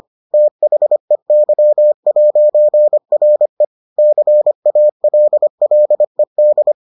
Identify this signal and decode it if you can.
Morse code, heard as a keyed tone, about 25 wpm, 600 Hz. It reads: THEY'RE CALLED